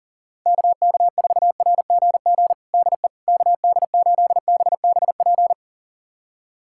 Morse code, heard as a keyed tone, about 37 wpm, 700 Hz. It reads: KK4RGG DE KD8BBP
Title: USA Callsign DE Callsign 40wpm